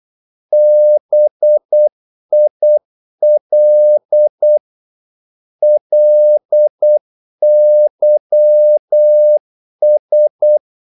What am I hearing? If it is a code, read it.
Morse code, heard as a keyed tone, about 8 wpm, 600 Hz. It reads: BIL LYS
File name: Korte ord DK 8wpm